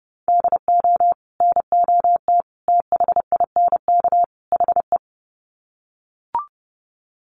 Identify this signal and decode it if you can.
Morse code, heard as a keyed tone, about 30 wpm, 700 Hz. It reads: DO NOT THINK HE
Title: Sets of 4 Words 30wpm